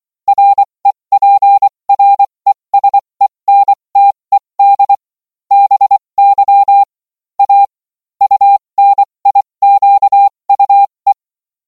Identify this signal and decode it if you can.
Morse code, heard as a keyed tone, about 24 wpm, spaced as for 20 wpm, 800 Hz. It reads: REPRESENTED BY A UNIQUE